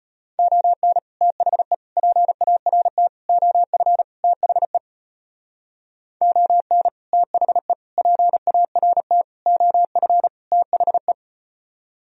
Morse code, heard as a keyed tone, about 37 wpm, 700 Hz. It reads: ON THE PART OF THE ON THE PART OF THE